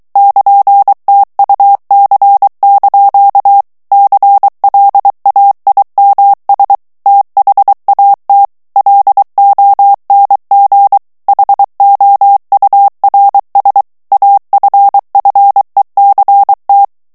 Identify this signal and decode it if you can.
Morse code, heard as a keyed tone, about 23 wpm, 800 Hz. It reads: <KN>TUC) CLAIMS THAT LONG HOURS AFFECT
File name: morse mp3 / ANGLAIS texte clair 1 vitesse 22 mots minute